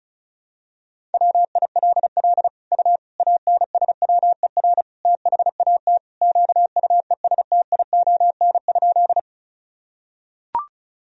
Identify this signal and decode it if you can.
Morse code, heard as a keyed tone, about 36 wpm, 700 Hz. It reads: WILL U ANSWER THAT QUESTION?
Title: Sentences from Top 400 Words and Abbreviations 35wpm